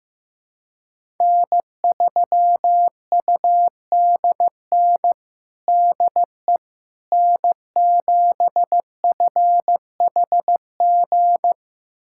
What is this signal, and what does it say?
Morse code, heard as a keyed tone, about 15 wpm, 700 Hz. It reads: N3UDN DE N7FHG
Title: USA Callsign DE Callsign - 5 Characters - 15wpm